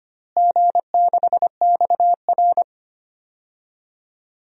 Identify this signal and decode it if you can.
Morse code, heard as a keyed tone, about 25 wpm, 700 Hz. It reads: G6XR
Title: DXCC-100 Call Signs - 4 Characters - Repeated 2x - 25wpm